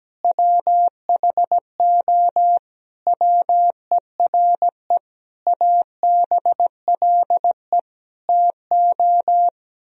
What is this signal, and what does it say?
Morse code, heard as a keyed tone, about 17 wpm, 700 Hz. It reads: WHO WERE ABLE TO
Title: Sentences from Top 1000 Words - Review Part 2 - 17wpm